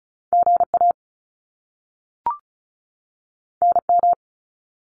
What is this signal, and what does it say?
Morse code, heard as a keyed tone, about 36 wpm, 700 Hz. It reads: GA NM
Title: US States 35wpm